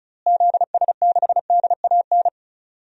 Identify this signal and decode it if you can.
Morse code, heard as a keyed tone, about 34 wpm, 700 Hz. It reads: ZS6DAN
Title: DXCC-100 Call Signs - 6 Characters - Repeated 3x - 35wpm